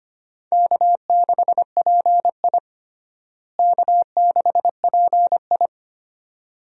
Morse code, heard as a keyed tone, about 25 wpm, 700 Hz. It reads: K6PI K6PI
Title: USA Call Signs - 4 Characters - Repeated 2x - 25wpm